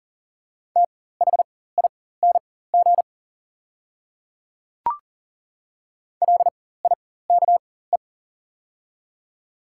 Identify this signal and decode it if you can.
Morse code, heard as a keyed tone, about 37 wpm, spaced as for 18 wpm, 700 Hz. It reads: THING LIKE
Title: 4x Character Spacing - Top 100 Words - 40wpm